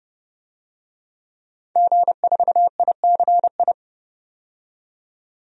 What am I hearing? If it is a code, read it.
Morse code, heard as a keyed tone, about 30 wpm, 700 Hz. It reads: G4ICI
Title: DXCC-100 Call Signs - 5 Characters - 30wpm